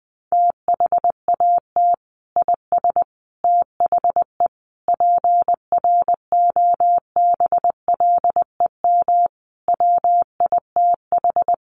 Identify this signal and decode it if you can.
Morse code, heard as a keyed tone, about 20 wpm, 700 Hz. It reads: THAT IS THE PROBLEM WITH
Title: Sentences from Top 1100 Words 20wpm